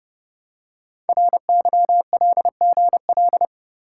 Morse code, heard as a keyed tone, about 30 wpm, 700 Hz. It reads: RYLGL
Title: Code Groups - 5 Letters - 30wpm